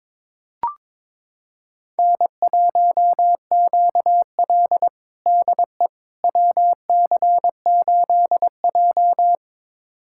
Morse code, heard as a keyed tone, about 22 wpm, 700 Hz. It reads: N1QL DE WC8J
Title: USA Callsign DE Callsign - 4 Characters - 22wpm